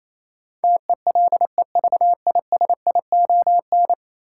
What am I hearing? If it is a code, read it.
Morse code, heard as a keyed tone, about 28 wpm, 700 Hz. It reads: TELEVISION
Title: Sentences from Top 1000 Words - Review Part 4 - New Word Drill - 28wpm